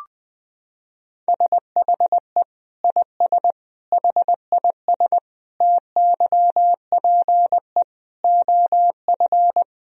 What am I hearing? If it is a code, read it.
Morse code, heard as a keyed tone, about 20 wpm, 700 Hz. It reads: SHE IS HIS TYPE OF
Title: Sentences from Top 1000 Words - Review Part 3 - 20wpm